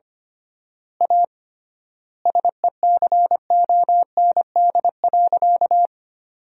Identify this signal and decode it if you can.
Morse code, heard as a keyed tone, about 25 wpm, 700 Hz. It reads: A SECOND.
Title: 3x Word-Spacing - Sentences from Top 500 Words - 25wpm